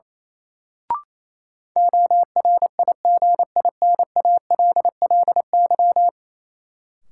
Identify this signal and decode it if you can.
Morse code, heard as a keyed tone, about 28 wpm, 700 Hz. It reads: ORIGINALLY
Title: Sentences from Top 1500 Words - New Word Drill Alt - 28wpm